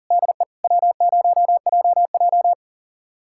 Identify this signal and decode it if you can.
Morse code, heard as a keyed tone, about 37 wpm, 700 Hz. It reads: DE W0JJ
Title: USA Callsign DE Callsign - 4 Characters - 40wpm